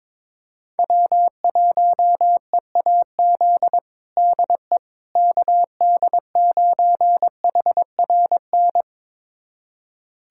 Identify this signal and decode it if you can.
Morse code, heard as a keyed tone, about 22 wpm, 700 Hz. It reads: W1EAZ DE KD9HRN
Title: USA Callsign DE Callsign 22wpm